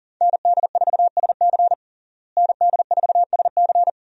Morse code, heard as a keyed tone, about 37 wpm, 700 Hz. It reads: ND4SC ND4SC
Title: USA Call Signs - 5 Characters - Repeated 2x - 40wpm